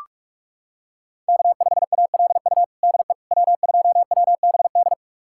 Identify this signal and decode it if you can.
Morse code, heard as a keyed tone, about 46 wpm, 700 Hz. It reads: K5ALU DE W2WBD
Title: USA Callsign DE Callsign - 5 Characters - 45wpm